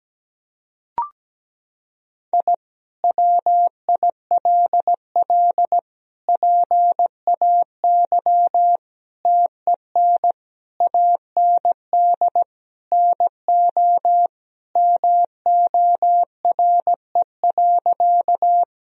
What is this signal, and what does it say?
Morse code, heard as a keyed tone, about 17 wpm, 700 Hz. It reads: I WILL PAY TEN AND NO MORE.
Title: Sentences from Top 1000 Words - Review Part 2 - 17wpm